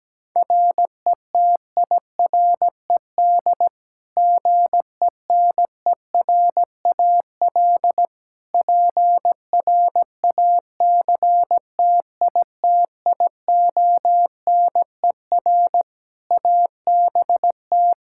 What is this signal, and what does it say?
Morse code, heard as a keyed tone, about 17 wpm, 700 Hz. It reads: RETIRED GENERAL PRACTITIONER ABT
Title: Practice QSO - Exchange 3 - More Info - 17wpm